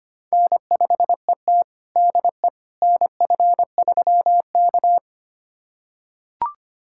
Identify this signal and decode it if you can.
Morse code, heard as a keyed tone, about 25 wpm, 700 Hz. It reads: N5ET DE NF3K
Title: USA Callsign DE Callsign - 4 Characters - 25wpm